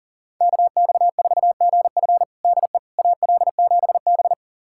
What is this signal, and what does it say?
Morse code, heard as a keyed tone, about 40 wpm, 700 Hz. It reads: KX4GF DE AL7B
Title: USA Callsign DE Callsign 40wpm